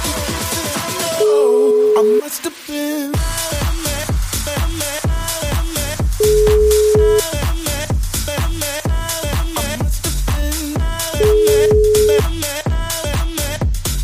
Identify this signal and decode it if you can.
Ringback tone (Europe, 425 Hz), with noise.